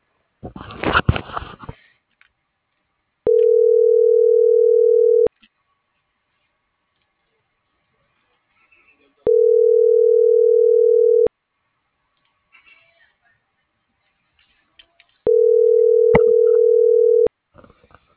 Ringback tone (North America), 440 + 480 Hz.